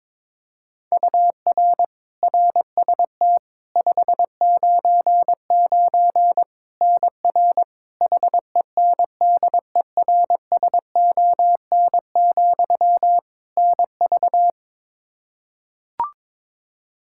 Morse code, heard as a keyed tone, about 22 wpm, 700 Hz. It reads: UR RST 599 NR HENDERSON, NV
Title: Practice QSO - Exchange 1 - Beginning - 22wpm